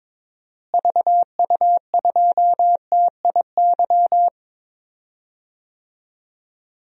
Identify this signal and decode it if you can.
Morse code, heard as a keyed tone, about 22 wpm, 700 Hz. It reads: VU2TIY
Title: DXCC-100 Call Signs - 6 Characters - Repeated 3x - 22wpm